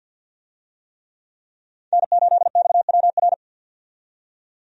Morse code, heard as a keyed tone, about 52 wpm, 700 Hz. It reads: N8XWR
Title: USA Call Signs - 5 Characters - 50wpm